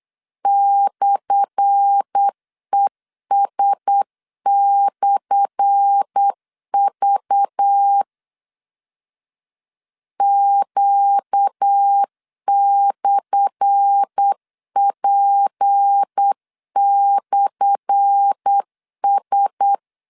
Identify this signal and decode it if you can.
Morse code, heard as a keyed tone, about 8 wpm, 800 Hz. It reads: /ES/V Q/P/S
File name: LEZIONI / 10 SIGNS / 10A SLASH 8wpm 3452